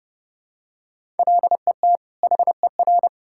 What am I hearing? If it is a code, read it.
Morse code, heard as a keyed tone, about 30 wpm, 700 Hz. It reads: LET HER
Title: Sentences from Top 1000 Words - Review Part 6 - 30wpm